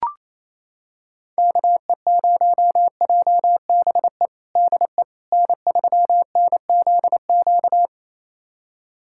Morse code, heard as a keyed tone, about 28 wpm, 700 Hz. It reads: KE0JBE DE N3NZQ
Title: USA Callsign DE Callsign 28wpm